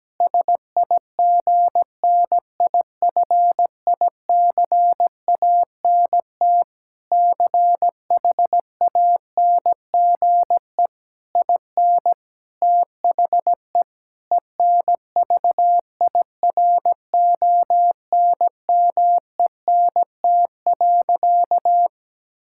Morse code, heard as a keyed tone, about 17 wpm, 700 Hz. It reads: SIGNIFICANT CHANGE IN THE ENVIRONMENT.
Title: Sentences from Top 1000 Words - Review Part 6 - 17wpm